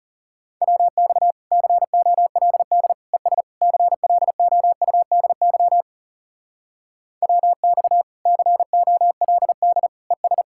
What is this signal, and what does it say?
Morse code, heard as a keyed tone, about 38 wpm, 700 Hz. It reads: WX COLD ES CLOUDY WX COLD ES
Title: Speed-Racing Common QSO Phrases - 30wpm